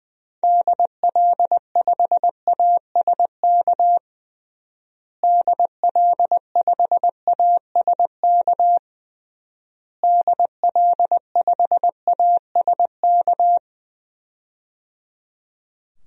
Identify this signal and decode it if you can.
Morse code, heard as a keyed tone, about 20 wpm, 700 Hz. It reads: DL5ASK DL5ASK DL5ASK
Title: DXCC-100 Call Signs - 6 Characters - Repeated 3x - 20wpm